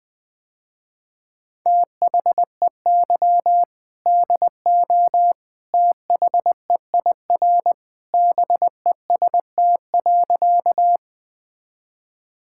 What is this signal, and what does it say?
Morse code, heard as a keyed tone, about 20 wpm, 700 Hz. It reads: THEY DO THEIR BEST.
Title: Sentences with 2-Word Phrases - Spoken Beginning - 20wpm